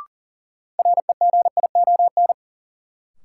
Morse code, heard as a keyed tone, about 40 wpm, 700 Hz. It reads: REGION
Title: Sentences from Top 1500 Words - New Word Drill - 40wpm